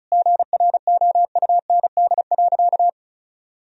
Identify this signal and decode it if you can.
Morse code, heard as a keyed tone, about 36 wpm, 700 Hz. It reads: GROUND.
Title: Sentences from Top 1000 Words - Review Part 2 - 35wpm